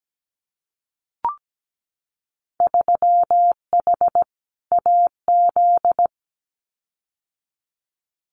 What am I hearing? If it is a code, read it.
Morse code, heard as a keyed tone, about 17 wpm, 700 Hz. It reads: 3H AZ
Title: Winter Field Day Exchanges Encore 17wpm